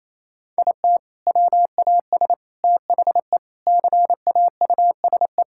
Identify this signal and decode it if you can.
Morse code, heard as a keyed tone, about 28 wpm, 700 Hz. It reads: IT WAS THE CAUSE
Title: Sentences from Top 1600 Words 28wpm